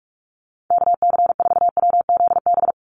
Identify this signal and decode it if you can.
Morse code, heard as a keyed tone, about 44 wpm, 700 Hz. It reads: KC4WZB
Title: NA SOTA Activator Callsigns 45wpm